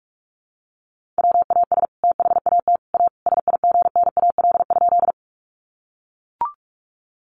Morse code, heard as a keyed tone, about 44 wpm, 700 Hz. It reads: WAS THAT A SIGNAL?
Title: Sentences from Top 1400 Words 45wpm